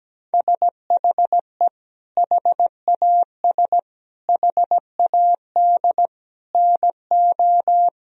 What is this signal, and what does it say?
Morse code, heard as a keyed tone, about 17 wpm, 700 Hz. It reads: SHE HAS HAD NO
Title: Sentences from Top 1000 Words - Review Part 4 - 17wpm